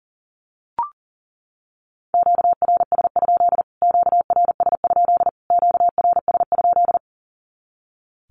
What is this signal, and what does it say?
Morse code, heard as a keyed tone, about 37 wpm, 700 Hz. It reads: QRS? QRS? QRS?
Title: Q-Codes Repeated 3x 40wpm